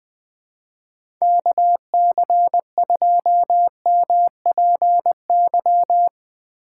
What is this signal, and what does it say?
Morse code, heard as a keyed tone, about 20 wpm, 700 Hz. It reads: KC2MPY
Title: USA Call Signs - 6 Characters - 3x Character Spacing - 20wpm